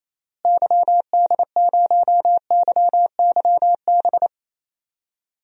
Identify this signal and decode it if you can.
Morse code, heard as a keyed tone, about 28 wpm, 700 Hz. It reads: YD0YYB
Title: DXCC-100 Call Signs - 6 Characters - Repeated 3x - 28wpm